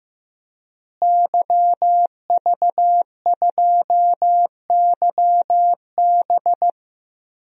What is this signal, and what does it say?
Morse code, heard as a keyed tone, about 15 wpm, 700 Hz. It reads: YV2YB
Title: DXCC-100 Call Signs - 5 Characters - Repeated 3x - 15wpm